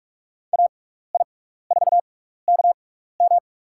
Morse code, heard as a keyed tone, about 44 wpm, spaced as for 15 wpm, 700 Hz. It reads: AI4KM
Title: USA Call Signs - 5 Characters - 6x Character Spacing - 45wpm